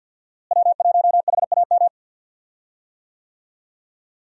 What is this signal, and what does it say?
Morse code, heard as a keyed tone, about 49 wpm, 700 Hz. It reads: W1HAM
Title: USA Call Signs - 5 Characters - Repeated 2x - 50wpm